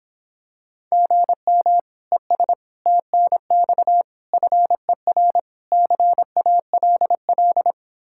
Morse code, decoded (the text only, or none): GM ES TNX FER CALL